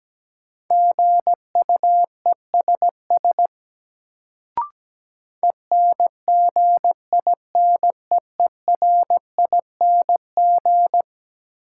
Morse code, heard as a keyed tone, about 17 wpm, 700 Hz. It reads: GUESS ENGINEERING
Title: Sentences from Top 1500 Words - New Word Drill - 17wpm